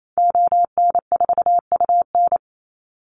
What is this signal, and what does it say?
Morse code, heard as a keyed tone, about 28 wpm, 700 Hz. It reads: ON4UN
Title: Call Signs - 28wpm